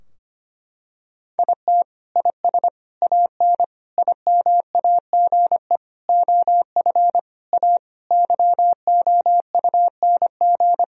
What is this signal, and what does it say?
Morse code, heard as a keyed tone, about 25 wpm, 700 Hz. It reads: IT IS AN IMAGE OF A YOUNG
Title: Sentences from Top 1000 Words - Review Part 4 - 25wpm